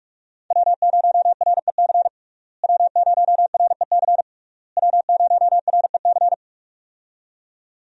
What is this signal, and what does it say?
Morse code, heard as a keyed tone, about 46 wpm, 700 Hz. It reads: W0REC W0REC W0REC
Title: USA Call Signs - 5 Characters - Repeated 3x - 45wpm